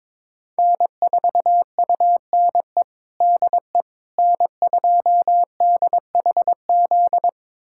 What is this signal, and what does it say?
Morse code, heard as a keyed tone, about 22 wpm, 700 Hz. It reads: N4UNE DE N2DHZ
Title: USA Callsign DE Callsign - 5 Characters - 22wpm